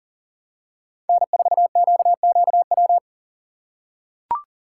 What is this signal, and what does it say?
Morse code, heard as a keyed tone, about 40 wpm, 700 Hz. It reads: N4QQW
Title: USA Call Signs - 5 Characters - 3x Character Spacing - 40wpm